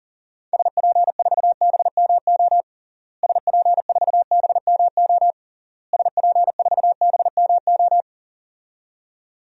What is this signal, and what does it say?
Morse code, heard as a keyed tone, about 40 wpm, 700 Hz. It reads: SP4BMO SP4BMO SP4BMO
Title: DXCC-100 Call Signs - 6 Characters - Repeated 3x - 40wpm